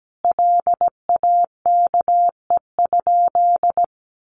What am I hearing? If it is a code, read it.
Morse code, heard as a keyed tone, about 17 wpm, 700 Hz. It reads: LAKE?
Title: Sentences from Top 1100 Words 17wpm